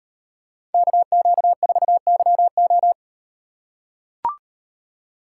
Morse code, heard as a keyed tone, about 38 wpm, 700 Hz. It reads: KQ4YO